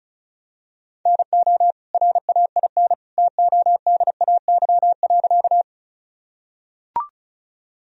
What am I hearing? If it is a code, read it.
Morse code, heard as a keyed tone, about 36 wpm, 700 Hz. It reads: NO RAIN TODAY.
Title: Sentences from Top 1000 Words - Review Part 1 - 35wpm